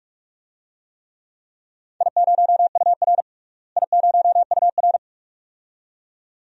Morse code, heard as a keyed tone, about 46 wpm, 700 Hz. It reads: I0UR I0UR
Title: DXCC-100 Call Signs - 4 Characters - Repeated 2x - 45wpm